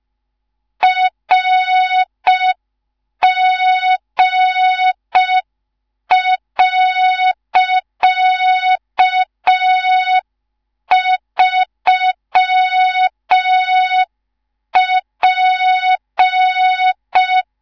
Morse code, heard as a keyed tone, about 5 wpm, 3700 Hz. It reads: RG.3P
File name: Morze-HA2MN / hang / 19 GYAKORLAT